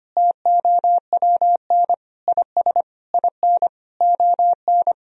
Morse code, heard as a keyed tone, about 25 wpm, 700 Hz. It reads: TOWN IS IN ON
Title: Sentences from Top 1000 Words - Review Part 4 - 25wpm